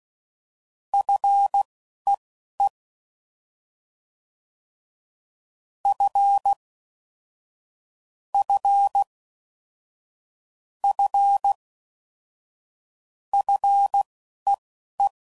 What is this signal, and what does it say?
Morse code, heard as a keyed tone, about 16 wpm, spaced as for 11 wpm, 800 Hz. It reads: FEE F F F FEE
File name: Prefixes / FLs & THs / F Prefixes